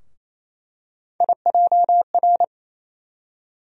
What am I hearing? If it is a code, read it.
Morse code, heard as a keyed tone, about 28 wpm, 700 Hz. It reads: IJR
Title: USA Call Sign Postfixes - 3 Letters 28wpm